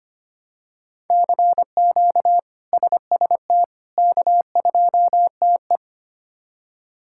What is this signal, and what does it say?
Morse code, heard as a keyed tone, about 25 wpm, 700 Hz. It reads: CQ SST K2TE
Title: K1USN SST QSO Practice - 25wpm